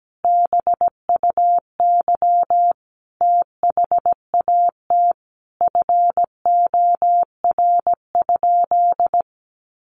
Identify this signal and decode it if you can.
Morse code, heard as a keyed tone, about 17 wpm, 700 Hz. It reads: BUY THAT FOR?
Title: Sentences from Top 700 Words 17wpm